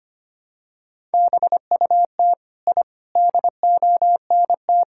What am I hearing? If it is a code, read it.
Morse code, heard as a keyed tone, about 25 wpm, 700 Hz. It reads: BUT I DONT
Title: Sentences from Top 1000 Words - Review Part 6 - 25wpm